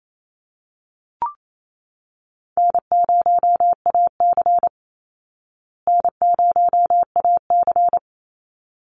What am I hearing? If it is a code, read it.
Morse code, heard as a keyed tone, about 28 wpm, 700 Hz. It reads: N0AC N0AC
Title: Easy Call Signs Repeated 2x - 28wpm